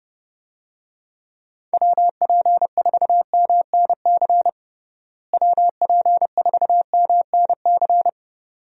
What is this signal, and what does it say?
Morse code, heard as a keyed tone, about 32 wpm, 700 Hz. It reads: WP4MNC WP4MNC
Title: USA Call Signs - 6 Characters - Repeated 2x - 30wpm